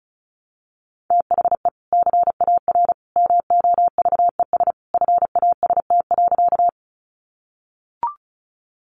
Morse code, heard as a keyed tone, about 34 wpm, 700 Hz. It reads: THE CAR MOVES FAST.
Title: Easy Phrases 35wpm